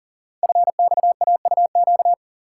Morse code, heard as a keyed tone, about 42 wpm, 690 Hz. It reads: FXAUQ
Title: Code Groups - 5 Letters - 40wpm